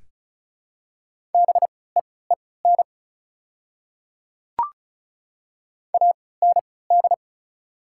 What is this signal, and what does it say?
Morse code, heard as a keyed tone, about 36 wpm, spaced as for 20 wpm, 690 Hz. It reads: BEEN AND
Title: 3x Character Spacing - Top 100 Words - 35wpm